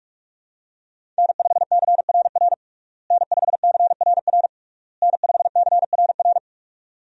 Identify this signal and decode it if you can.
Morse code, heard as a keyed tone, about 46 wpm, 700 Hz. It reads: N5CRR N5CRR N5CRR
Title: USA Call Signs - 5 Characters - Repeated 3x - 45wpm